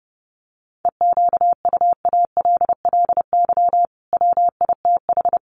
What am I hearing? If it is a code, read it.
Morse code, heard as a keyed tone, about 30 wpm, 700 Hz. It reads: EQUALLY WITH
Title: Sentences from Top 1400 Words 30wpm